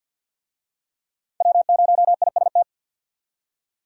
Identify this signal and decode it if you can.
Morse code, heard as a keyed tone, about 50 wpm, 700 Hz. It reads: W0IST
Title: USA Call Signs - 5 Characters - 50wpm